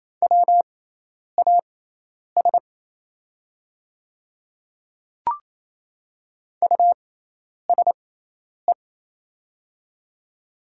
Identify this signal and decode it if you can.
Morse code, heard as a keyed tone, about 28 wpm, spaced as for 10 wpm, 700 Hz. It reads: WAS USE
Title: 6x Character Spacing - Top 100 Words - 28wpm